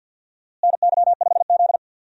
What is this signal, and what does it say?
Morse code, heard as a keyed tone, about 51 wpm, 700 Hz. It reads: NY5Z
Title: USA Call Signs - 4 Characters - Repeated 3x - 50wpm